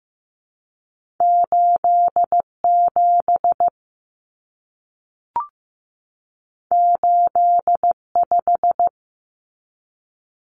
Morse code, heard as a keyed tone, about 15 wpm, 700 Hz. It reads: 87 85
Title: Common Cut Numbers - 2-Digit 15wpm